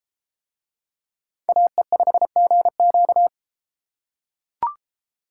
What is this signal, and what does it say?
Morse code, heard as a keyed tone, about 32 wpm, 700 Hz. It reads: AE5GQ